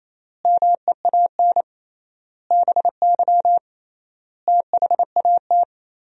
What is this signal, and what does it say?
Morse code, heard as a keyed tone, about 28 wpm, 700 Hz. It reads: MEAN BY THAT
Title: 3x Word-Spacing - Sentences from Top 400 Words - 28wpm